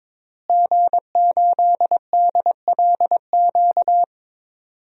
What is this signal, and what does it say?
Morse code, heard as a keyed tone, about 22 wpm, 700 Hz. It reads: G8DLQ
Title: DXCC-100 Call Signs - 5 Characters - Repeated 2x - 22wpm